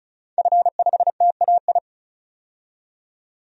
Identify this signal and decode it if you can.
Morse code, heard as a keyed tone, about 36 wpm, 700 Hz. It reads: F5TAI
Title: DXCC-100 Call Signs - 5 Characters - 35wpm